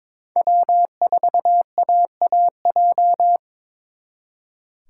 Morse code, heard as a keyed tone, about 28 wpm, 700 Hz. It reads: W4AAJ
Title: Speed Racing - USA 1x3 Call Signs 22wpm